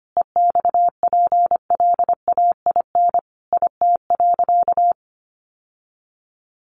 Morse code, heard as a keyed tone, about 25 wpm, 700 Hz. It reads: EXPLAIN IT.
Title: Sentences from Top 1100 Words 25wpm